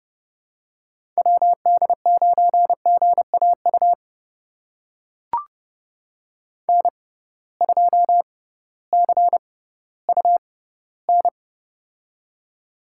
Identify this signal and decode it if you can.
Morse code, heard as a keyed tone, about 30 wpm, spaced as for 16 wpm, 700 Hz. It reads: WD9GAU N2CUN
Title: USA Call Signs - 6x Character Spacing - 30wpm